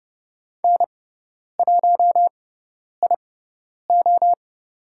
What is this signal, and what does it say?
Morse code, heard as a keyed tone, about 30 wpm, spaced as for 10 wpm, 700 Hz. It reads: N1IO
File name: Lesson 18g - Letter m - Call Signs 30wpm